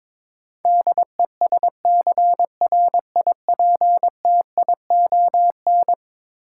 Morse code, heard as a keyed tone, about 22 wpm, 700 Hz. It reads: DESCRIPTION
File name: practice / 22wpm / Eleven Letter Words - 22wpm